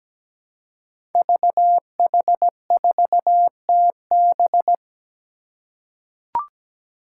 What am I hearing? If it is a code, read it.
Morse code, heard as a keyed tone, about 17 wpm, 700 Hz. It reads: VH4TB
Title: Five Challenging Letter-Number Combinations 17wpm